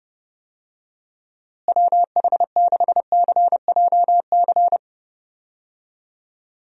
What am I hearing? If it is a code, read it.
Morse code, heard as a keyed tone, about 30 wpm, 700 Hz. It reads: WH6CJC
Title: USA Call Signs - 6 Characters - Repeated 3x - 30wpm